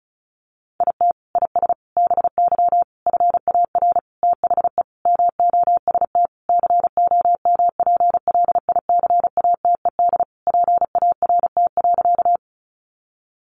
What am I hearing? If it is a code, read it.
Morse code, heard as a keyed tone, about 34 wpm, 700 Hz. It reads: IT IS BY FAR THE MOST COMPLICATED PART.
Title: Sentences from Top 1200 Words 35wpm